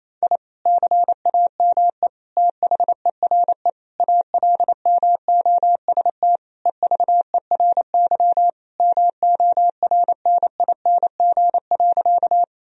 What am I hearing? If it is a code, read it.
Morse code, heard as a keyed tone, about 27 wpm, 700 Hz. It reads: I CAME THERE ALMOST EVERY MORNING.
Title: Sentences from Top 1000 Words - Review Part 4 - 28wpm